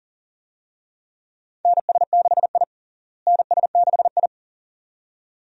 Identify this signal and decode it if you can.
Morse code, heard as a keyed tone, about 40 wpm, 700 Hz. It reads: NS6I NS6I
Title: USA Call Signs - 4 Characters - Repeated 2x - 40wpm